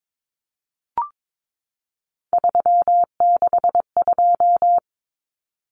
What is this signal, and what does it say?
Morse code, heard as a keyed tone, about 22 wpm, 700 Hz. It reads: 362
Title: Common Cut Numbers - 3-Digit 22wpm